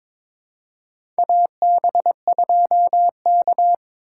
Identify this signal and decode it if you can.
Morse code, heard as a keyed tone, about 22 wpm, 700 Hz. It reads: AB2K
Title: USA Call Signs - 4 Characters - 3x Character Spacing - 22wpm